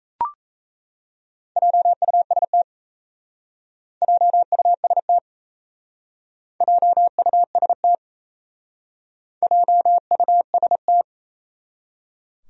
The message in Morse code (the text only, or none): JUST JUST JUST JUST